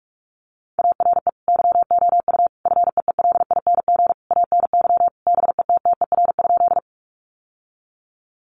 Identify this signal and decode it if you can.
Morse code, heard as a keyed tone, about 44 wpm, 700 Hz. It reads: ARE YOU FEELING ANY BETTER?
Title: Sentences from Top 1200 Words 45wpm